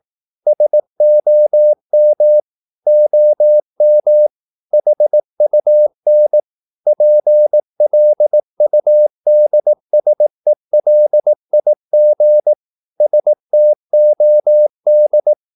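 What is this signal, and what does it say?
Morse code, heard as a keyed tone, about 18 wpm, 600 Hz. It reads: SOM OM HUN PLUDSELIG STOD
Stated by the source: Slow (QRS) Morse practice at 18 wpm, CW med Gnister.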